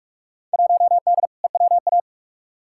Morse code, heard as a keyed tone, about 46 wpm, 700 Hz. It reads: 1D EWA
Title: Field Day Exchanges - 45wpm